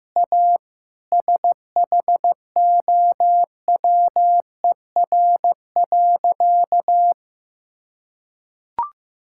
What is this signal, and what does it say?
Morse code, heard as a keyed tone, about 16 wpm, 700 Hz. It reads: A SHOWER.
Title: Sentences from Top 1600 Words 15wpm